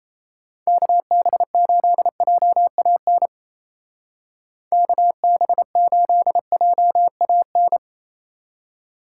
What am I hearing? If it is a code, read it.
Morse code, heard as a keyed tone, about 35 wpm, 700 Hz. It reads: KB8JAN KB8JAN